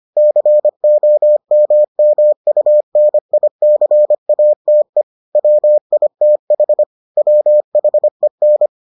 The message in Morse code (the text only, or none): COMMUNICATE WITH WHEN